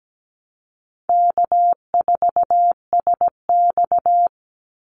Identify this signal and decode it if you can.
Morse code, heard as a keyed tone, about 17 wpm, 700 Hz. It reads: K4SX
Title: Easy Call Signs Repeated 2x - 17wpm